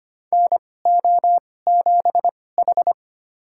Morse code, heard as a keyed tone, about 25 wpm, spaced as for 18 wpm, 700 Hz. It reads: NO7H
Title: USA Call Signs - 4 Characters - 2x Character Spacing - 25wpm